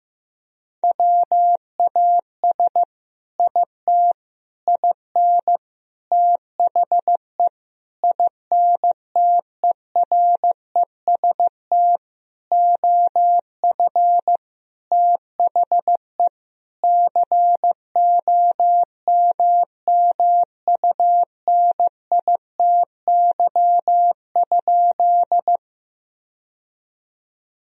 Morse code, heard as a keyed tone, about 15 wpm, 700 Hz. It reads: WAS IT IN THE INTEREST OF THE COMMUNITY?
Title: Sentences from Top 1000 Words - Review Part 4 - 15wpm